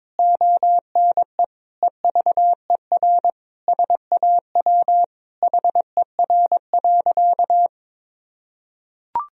Morse code, heard as a keyed tone, about 22 wpm, 700 Hz. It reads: ONE EVER SAW HER.
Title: Sentences from Top 1000 Words - Review Part 2 - 22wpm